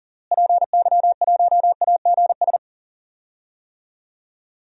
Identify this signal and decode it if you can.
Morse code, heard as a keyed tone, about 40 wpm, 700 Hz. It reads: PY1AGS
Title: DXCC-100 Call Signs - 6 Characters - Repeated 2x - 40wpm